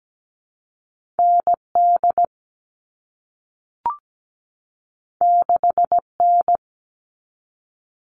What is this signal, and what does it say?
Morse code, heard as a keyed tone, about 17 wpm, 700 Hz. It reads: ND 6N
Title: All Cut Numbers - 2-Digit 17wpm